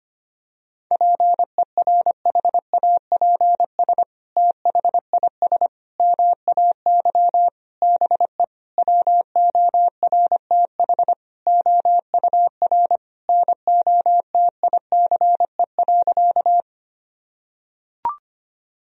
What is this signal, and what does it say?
Morse code, heard as a keyed tone, about 25 wpm, 700 Hz. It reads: PERHAPS THIS MAY BE WORTH OUR NOTICE.
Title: Sentences from Top 1000 Words - Review Part 2 - 25wpm